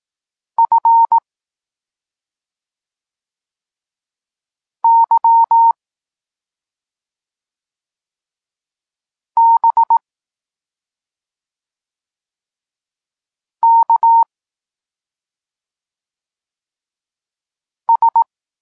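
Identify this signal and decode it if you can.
Morse code, heard as a keyed tone, about 18 wpm, 920 Hz. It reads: F Y B K S
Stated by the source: Polish Morse lesson teaching the letter C, from SP9MOA Niepołomicki Klub Krótkofalowców.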